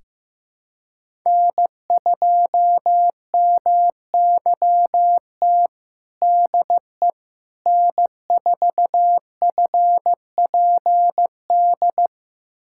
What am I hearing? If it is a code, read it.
Morse code, heard as a keyed tone, about 15 wpm, 700 Hz. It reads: N2MYT DE N4FPD
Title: USA Callsign DE Callsign - 5 Characters - 15wpm